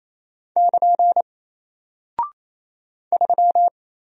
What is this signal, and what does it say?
Morse code, heard as a keyed tone, about 28 wpm, 700 Hz. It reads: <KN> 3
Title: Single Character (Letter:Number:Punctuation:Prosign) Alt 28wpm